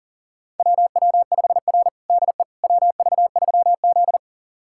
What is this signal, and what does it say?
Morse code, heard as a keyed tone, about 40 wpm, 700 Hz. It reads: WW5R DE WV3Z
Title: USA Callsign DE Callsign - 4 Characters - 40wpm